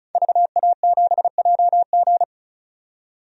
Morse code, heard as a keyed tone, about 34 wpm, 700 Hz. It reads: VA7JG
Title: DXCC-100 Call Signs - 5 Characters - 35wpm